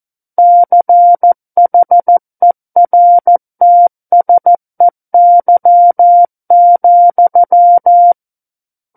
Morse code, heard as a keyed tone, about 14 wpm, 700 Hz. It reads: CHERTSEY,
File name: Book One / 05-Chapter 5 - 14 WPM